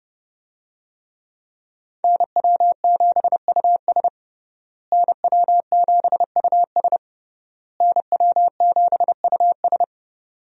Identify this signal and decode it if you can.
Morse code, heard as a keyed tone, about 32 wpm, 700 Hz. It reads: NW7US NW7US NW7US
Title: USA Call Signs - 5 Characters - Repeated 3x - 30wpm